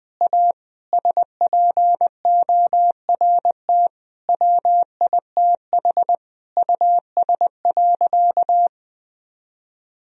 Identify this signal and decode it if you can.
Morse code, heard as a keyed tone, about 21 wpm, 700 Hz. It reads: A SPORT WITH US.
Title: Sentences from Top 1000 Words - Review Part 4 - 20wpm